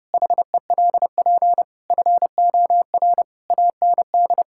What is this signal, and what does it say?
Morse code, heard as a keyed tone, about 32 wpm, 700 Hz. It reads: HELP FOR AND
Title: Pangram Challenge v2 - 30wpm